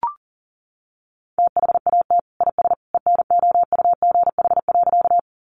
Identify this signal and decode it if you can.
Morse code, heard as a keyed tone, about 40 wpm, 700 Hz. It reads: THAT IS ENOUGH.
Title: Sentences from Top 400 Words 40wpm